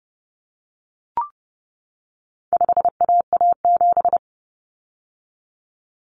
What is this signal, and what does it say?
Morse code, heard as a keyed tone, about 30 wpm, 700 Hz. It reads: 5AA7
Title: Common Cut Numbers - 4-Digit 30wpm